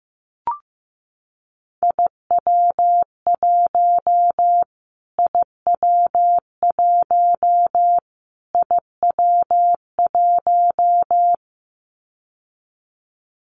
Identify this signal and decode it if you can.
Morse code, heard as a keyed tone, about 15 wpm, 700 Hz. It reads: IW1 IW1 IW1
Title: Callsign Prefixes for Top 25 DXCC - Repeated 3x - 15wpm